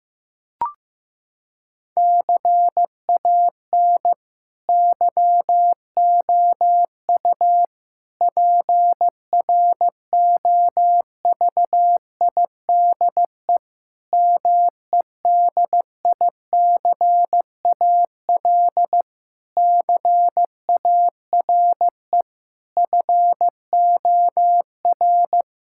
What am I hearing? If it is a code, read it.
Morse code, heard as a keyed tone, about 16 wpm, 700 Hz. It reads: CAN YOU PROVIDE MEDICAL CARE FOR
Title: Sentences from Top 1000 Words - Review Part 5 - 15wpm